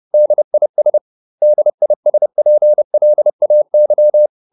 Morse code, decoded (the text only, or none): DIS DISPLAY